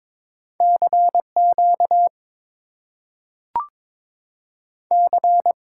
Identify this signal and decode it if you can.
Morse code, heard as a keyed tone, about 22 wpm, 700 Hz. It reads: CQ C
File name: Top 90 QSO Elements - 8x Character Spacing - 22wpm